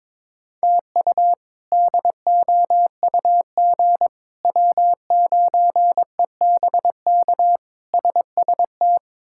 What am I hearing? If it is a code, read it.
Morse code, heard as a keyed tone, about 22 wpm, 700 Hz. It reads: TU DOUG W9EBK SST
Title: K1USN SST QSO Practice - 22wpm